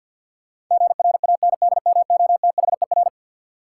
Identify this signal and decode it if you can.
Morse code, heard as a keyed tone, about 51 wpm, 700 Hz. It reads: GRANDMOTHER
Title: Sentences from Top 1200 Words - New Word Drill Alt - 50wpm